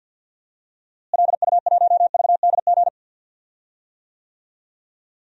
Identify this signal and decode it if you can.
Morse code, heard as a keyed tone, about 50 wpm, 700 Hz. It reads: LU1VDG
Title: DXCC-100 Call Signs - 6 Characters - Repeated 2x - 50wpm